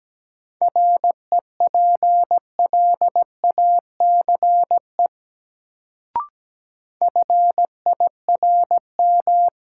Morse code, heard as a keyed tone, about 17 wpm, 700 Hz. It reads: REPLACE FIRM
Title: Sentences from Top 1400 Words - New Word Drill - 17wpm